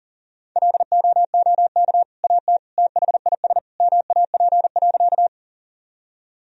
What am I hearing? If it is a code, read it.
Morse code, heard as a keyed tone, about 40 wpm, 700 Hz. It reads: LOOK AT THIS MAP.
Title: Sentences from Top 1000 Words - Review Part 2 - 40wpm